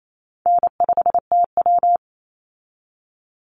Morse code, heard as a keyed tone, about 28 wpm, 700 Hz. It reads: N5TW
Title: Easy Call Signs - 28wpm